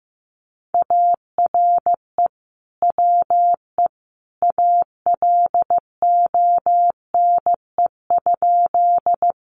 Morse code, heard as a keyed tone, about 15 wpm, 700 Hz. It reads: ARE WE ALONE?
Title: Sentences from Top 1000 Words 15wpm